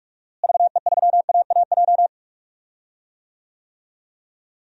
Morse code, heard as a keyed tone, about 42 wpm, 700 Hz. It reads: VE3AAJ
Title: DXCC-100 Call Signs - 6 Characters - Repeated 3x - 45wpm